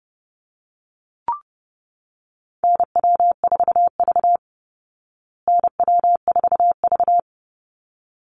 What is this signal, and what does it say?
Morse code, heard as a keyed tone, about 30 wpm, 700 Hz. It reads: NW4V NW4V
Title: Call Signs Repeated 3x - 30wpm